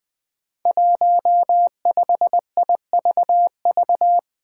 Morse code, heard as a keyed tone, about 20 wpm, 700 Hz. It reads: 15IVV
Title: Five Challenging Letter-Number Combinations 20wpm